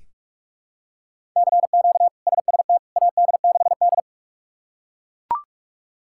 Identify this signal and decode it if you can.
Morse code, heard as a keyed tone, about 42 wpm, 700 Hz. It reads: CQ SST AD6D
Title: K1USN SST QSO Practice - 45wpm